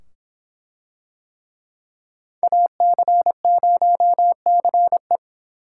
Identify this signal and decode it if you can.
Morse code, heard as a keyed tone, about 24 wpm, 700 Hz. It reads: AC0CE